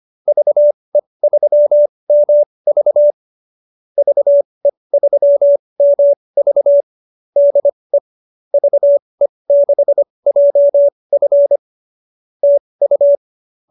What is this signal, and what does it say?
Morse code, heard as a keyed tone, about 25 wpm, spaced as for 20 wpm, 590 Hz. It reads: VE3MV VE3MV DE VE6JF TU